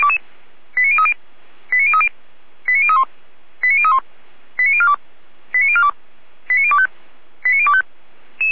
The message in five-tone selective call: 89030 89030 89031 89031 89042 89042 89025 89025

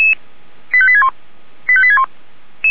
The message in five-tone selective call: 85861 85861

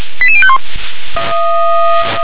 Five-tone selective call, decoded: 88051

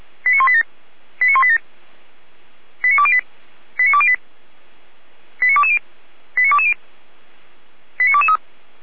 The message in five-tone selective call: 89187 89187 89208 89208 89220 89220 89223

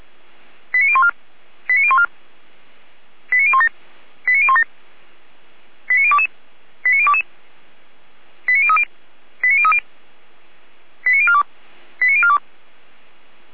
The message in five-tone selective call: 89014 89014 89017 89017 89022 89022 89030 89030 89042 89042

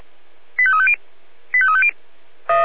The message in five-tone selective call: 85350 85350

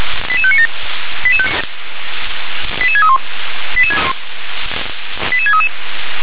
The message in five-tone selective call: 88597 88631 88633

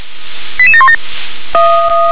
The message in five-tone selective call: 88717

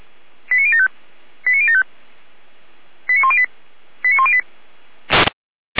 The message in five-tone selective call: 89075 89075 89108 89108

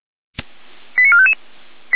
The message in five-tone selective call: 89455

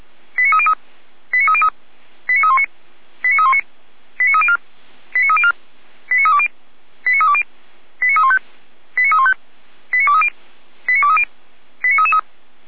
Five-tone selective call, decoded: 89303 89303 89319 89319 89304 89304 89320 89320 89315 89315 89230 89230 89333

